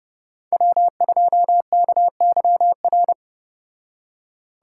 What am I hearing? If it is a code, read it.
Morse code, heard as a keyed tone, about 34 wpm, 700 Hz. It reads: W2KYR